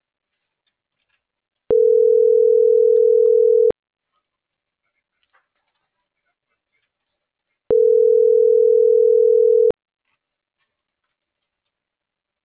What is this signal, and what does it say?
Ringback tone (North America), 440 + 480 Hz.